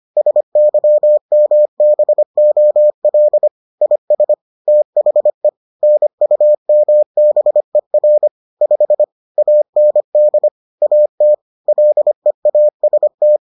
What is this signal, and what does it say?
Morse code, heard as a keyed tone, about 25 wpm, 600 Hz. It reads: SYMBOL IS THE NUMBER 5 AND AT LEAST